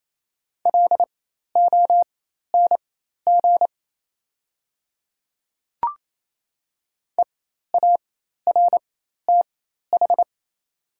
Morse code, heard as a keyed tone, about 28 wpm, spaced as for 13 wpm, 690 Hz. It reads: LONG EARTH
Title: 4x Character Spacing - Top 200 Words - 28wpm